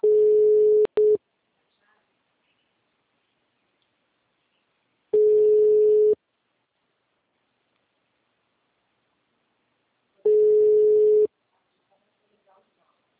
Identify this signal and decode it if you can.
Ringback tone (Europe, 425 Hz), with speech.
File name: out- -2010- - - .6004